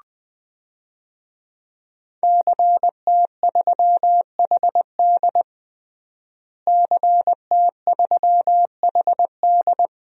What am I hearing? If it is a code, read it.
Morse code, heard as a keyed tone, about 20 wpm, 700 Hz. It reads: CT3HD CT3HD
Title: DXCC-100 Call Signs - 5 Characters - Repeated 2x - 20wpm